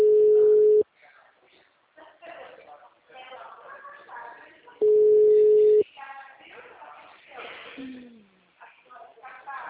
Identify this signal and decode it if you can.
Ringback tone (Europe, 425 Hz), with speech.